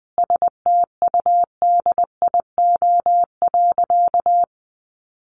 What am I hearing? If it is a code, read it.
Morse code, heard as a keyed tone, about 20 wpm, 700 Hz. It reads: STUDIO.
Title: Sentences from Top 1200 Words 20wpm